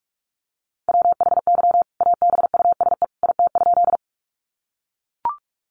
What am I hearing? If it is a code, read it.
Morse code, heard as a keyed tone, about 44 wpm, 700 Hz. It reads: WHY ABUSE IT?
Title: Sentences from Top 1300 Words 45wpm